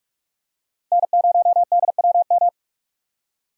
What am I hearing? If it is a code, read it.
Morse code, heard as a keyed tone, about 46 wpm, 700 Hz. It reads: N0DWM
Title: USA Call Signs - 5 Characters - 6x Character Spacing - 45wpm